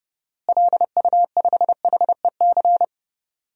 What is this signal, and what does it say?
Morse code, heard as a keyed tone, about 30 wpm, 700 Hz. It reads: LU5HEC
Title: DXCC-100 Call Signs - 6 Characters - Repeated 2x - 30wpm